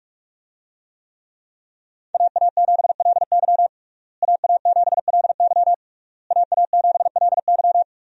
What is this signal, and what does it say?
Morse code, heard as a keyed tone, about 47 wpm, 700 Hz. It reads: AA7LY AA7LY AA7LY
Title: USA Call Signs - 5 Characters - Repeated 3x - 45wpm